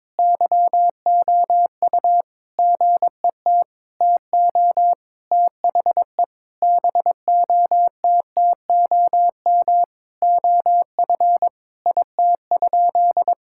Morse code, heard as a keyed tone, about 22 wpm, 700 Hz. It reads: YOU GET TO THE BOTTOM OF IT?
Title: Sentences from Top 1000 Words - Review Part 6 - 22wpm